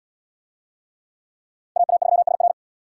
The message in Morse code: AA3SR

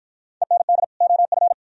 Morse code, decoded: END OF

